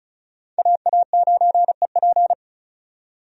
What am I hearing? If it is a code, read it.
Morse code, heard as a keyed tone, about 34 wpm, 700 Hz. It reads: AA9EP
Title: USA Call Signs - 5 Characters - 35wpm